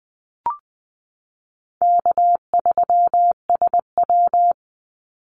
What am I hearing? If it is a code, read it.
Morse code, heard as a keyed tone, about 20 wpm, 700 Hz. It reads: K3SW
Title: Easy Call Signs Repeated 3x - 20wpm